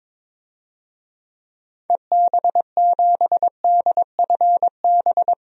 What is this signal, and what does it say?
Morse code, heard as a keyed tone, about 22 wpm, 700 Hz. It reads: EB7DFB
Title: DXCC-100 Call Signs - 6 Characters - 22wpm